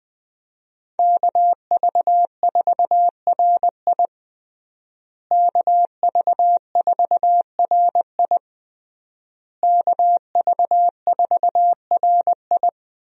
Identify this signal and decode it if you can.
Morse code, heard as a keyed tone, about 20 wpm, 700 Hz. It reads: KV4RI KV4RI KV4RI
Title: USA Call Signs - 5 Characters - Repeated 3x - 20wpm